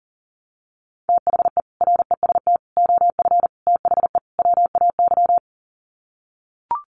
Morse code, heard as a keyed tone, about 40 wpm, 700 Hz. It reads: THE REST OF THE WAY
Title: Sets of 5 Words Encore 40wpm